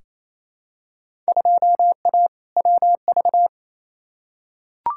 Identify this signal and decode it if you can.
Morse code, heard as a keyed tone, about 28 wpm, 700 Hz. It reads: 2A WV
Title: Field Day Exchanges Encore - 28wpm